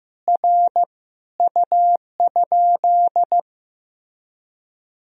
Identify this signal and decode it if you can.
Morse code, heard as a keyed tone, about 15 wpm, 700 Hz. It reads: R U?
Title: Sentences from Top 200 Words and Abbreviations 15wpm